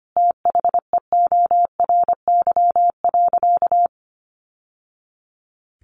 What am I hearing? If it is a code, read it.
Morse code, heard as a keyed tone, about 25 wpm, 700 Hz. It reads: THEORY.
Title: Sentences from Top 800 Words 25wpm